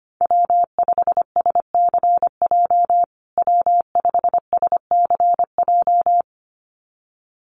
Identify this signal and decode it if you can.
Morse code, heard as a keyed tone, about 25 wpm, 700 Hz. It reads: W5SCJ W5SCJ
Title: USA Call Signs Repeated 2x 25wpm